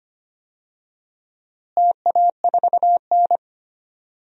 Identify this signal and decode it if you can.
Morse code, heard as a keyed tone, about 25 wpm, 700 Hz. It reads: TA4N
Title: DXCC-100 Call Signs - 4 Characters - Repeated 3x - 25wpm